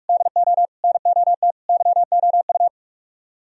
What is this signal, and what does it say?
Morse code, heard as a keyed tone, about 45 wpm, 700 Hz. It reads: DO NOT YOU